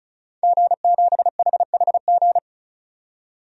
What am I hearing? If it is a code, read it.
Morse code, heard as a keyed tone, about 36 wpm, 700 Hz. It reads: G7HHG
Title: DXCC-100 Call Signs - 5 Characters - 35wpm